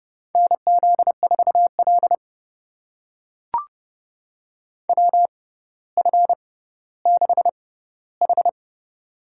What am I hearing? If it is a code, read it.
Morse code, heard as a keyed tone, about 30 wpm, spaced as for 15 wpm, 700 Hz. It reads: NZ4L WF6H
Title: USA Call Signs - 6x Character Spacing - 30wpm